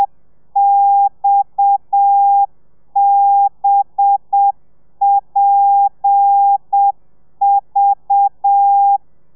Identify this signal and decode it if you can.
Morse code, heard as a keyed tone, about 7 wpm, 800 Hz. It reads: XBPV
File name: telegrafia / i2tfj / Lez7 30